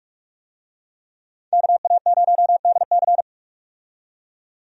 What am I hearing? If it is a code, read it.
Morse code, heard as a keyed tone, about 46 wpm, 700 Hz. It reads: KA0DC
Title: USA Call Signs - 5 Characters - 45wpm